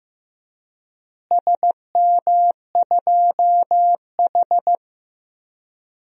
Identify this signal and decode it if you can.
Morse code, heard as a keyed tone, about 15 wpm, 700 Hz. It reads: SM2H
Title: DXCC-100 Call Signs - 4 Characters - Repeated 3x - 15wpm